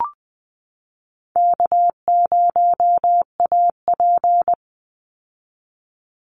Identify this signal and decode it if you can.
Morse code, heard as a keyed tone, about 20 wpm, 700 Hz. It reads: K0AP
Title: Easy Call Signs Repeated 3x - 20wpm